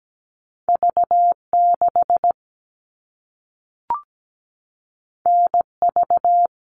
Morse code, heard as a keyed tone, about 17 wpm, 700 Hz. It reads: V6 NV
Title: All Cut Numbers - 2-Digit 17wpm